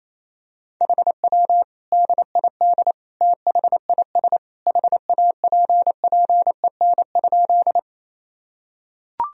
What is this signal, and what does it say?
Morse code, heard as a keyed tone, about 28 wpm, 700 Hz. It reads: HW DID THIS HAPPEN?
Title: Sentences from Top 400 Words and Abbreviations 28wpm